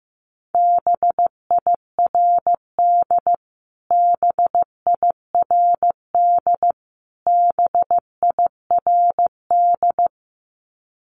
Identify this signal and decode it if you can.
Morse code, heard as a keyed tone, about 15 wpm, 700 Hz. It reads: BIRD BIRD BIRD
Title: Top 500 Words Repeated 15wpm